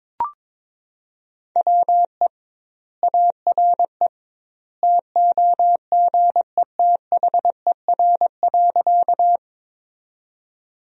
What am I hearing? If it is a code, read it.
Morse code, heard as a keyed tone, about 22 wpm, 700 Hz. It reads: WE ARE TOGETHER.
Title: 2x Word-Spacing - Sentences from Top 300 Words - 22wpm